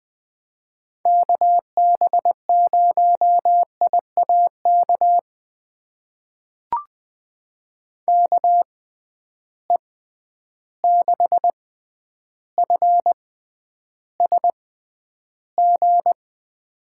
Morse code, heard as a keyed tone, about 21 wpm, spaced as for 10 wpm, 700 Hz. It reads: KB0IAK KE6FSG
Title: USA Call Signs - 6 Characters - 6x Character Spacing - 20wpm